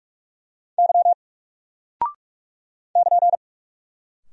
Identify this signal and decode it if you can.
Morse code, heard as a keyed tone, about 44 wpm, 700 Hz. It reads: Y <KN>
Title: Single Character (Letter:Number:Punctuation:Prosign) 45wpm